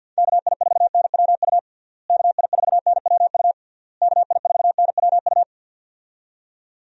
Morse code, heard as a keyed tone, about 51 wpm, 700 Hz. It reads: KI4NWU KI4NWU KI4NWU
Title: USA Call Signs - 6 Characters - Repeated 3x - 50wpm